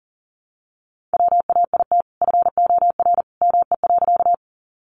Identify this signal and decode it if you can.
Morse code, heard as a keyed tone, about 40 wpm, 700 Hz. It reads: WAIT FOR ME.
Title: Sentences from Top 500 Words 40wpm